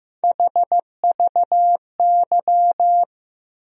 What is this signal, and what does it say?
Morse code, heard as a keyed tone, about 15 wpm, 700 Hz. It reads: HVY
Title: USA Call Sign Postfixes - 3 Letters 15wpm